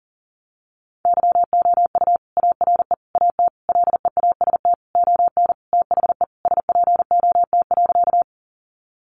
Morse code, heard as a keyed tone, about 40 wpm, 700 Hz. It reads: YOU ARE AT LEAST ON THE SPOT.
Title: Sentences from Top 1100 Words 40wpm